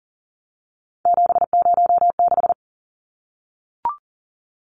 Morse code, heard as a keyed tone, about 40 wpm, 700 Hz. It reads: 706
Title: Numbers - 3-Digit 40wpm